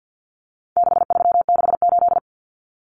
Morse code, heard as a keyed tone, about 50 wpm, 700 Hz. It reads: 6368
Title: Common Cut Numbers - 4-Digit 50wpm